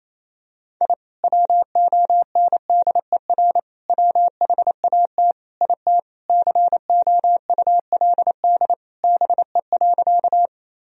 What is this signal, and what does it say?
Morse code, heard as a keyed tone, about 27 wpm, 700 Hz. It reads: I WONDER WHAT IT COULD BE.
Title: Sentences from Top 1000 Words - Review Part 3 - 28wpm